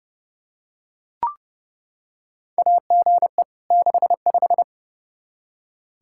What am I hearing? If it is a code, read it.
Morse code, heard as a keyed tone, about 29 wpm, 700 Hz. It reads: AGE 65
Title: K1USN SST QSO Practice 30wpm